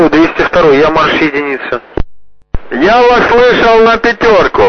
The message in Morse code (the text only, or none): none